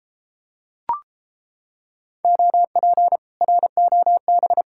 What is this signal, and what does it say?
Morse code, heard as a keyed tone, about 33 wpm, 700 Hz. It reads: OP ROB